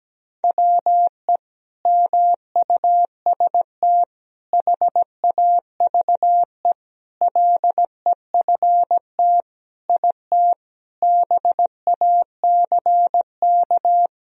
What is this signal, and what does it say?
Morse code, heard as a keyed tone, about 17 wpm, 700 Hz. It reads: WE MUST HAVE LEFT IT BACK
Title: Sentences from Top 1000 Words - Review Part 2 - 17wpm